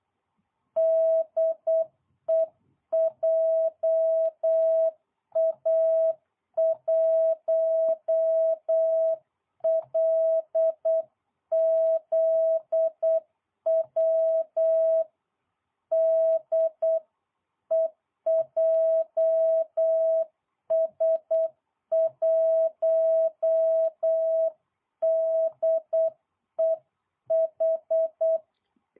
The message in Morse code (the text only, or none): DE JA1LZW DE JS1DEH